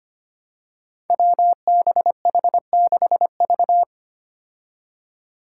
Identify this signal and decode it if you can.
Morse code, heard as a keyed tone, about 25 wpm, 700 Hz. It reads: WBH6V
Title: Five Challenging Letter-Number Combinations 25wpm